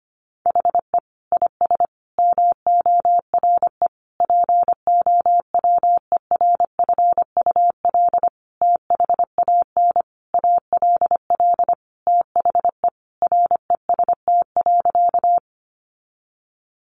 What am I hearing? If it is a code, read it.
Morse code, heard as a keyed tone, about 25 wpm, 700 Hz. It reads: HE IS MORE POWERFUL THAN ALL THE REST.
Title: Sentences from Top 1300 Words 25wpm